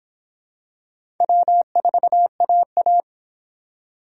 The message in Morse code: W4AA

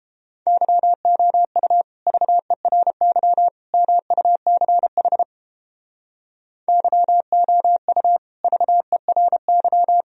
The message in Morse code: YOU VERY MUCH YOU VERY

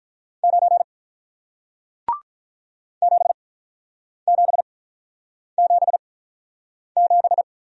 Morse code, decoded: 9 7777

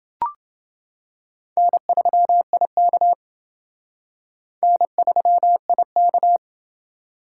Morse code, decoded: N3IK N3IK

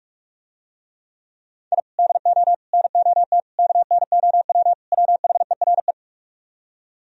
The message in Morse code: I DO NOT KNOW WHERE

